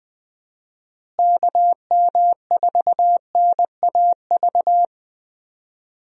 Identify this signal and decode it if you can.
Morse code, heard as a keyed tone, about 20 wpm, 700 Hz. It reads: KM4NAV
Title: USA Call Signs - 6 Characters - Repeated 3x - 20wpm